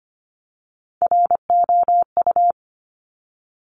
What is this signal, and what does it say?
Morse code, heard as a keyed tone, about 25 wpm, 700 Hz. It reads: ROU
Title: Top Three Letter Combinations in Words 25wpm